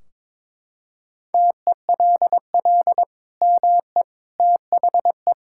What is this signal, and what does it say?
Morse code, heard as a keyed tone, about 21 wpm, 700 Hz. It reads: TELL ME THE
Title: Sentences from Top 1600 Words 22wpm